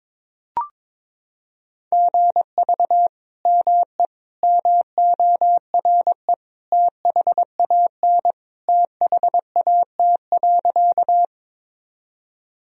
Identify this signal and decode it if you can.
Morse code, heard as a keyed tone, about 22 wpm, 700 Hz. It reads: GV ME MORE THAN THAT.
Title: Sentences from Top 200 Words and Abbreviations 22wpm